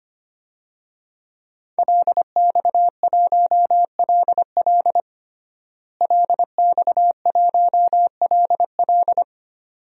Morse code, heard as a keyed tone, about 25 wpm, 700 Hz. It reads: LX1LL LX1LL
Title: DXCC-100 Call Signs - 5 Characters - Repeated 2x - 25wpm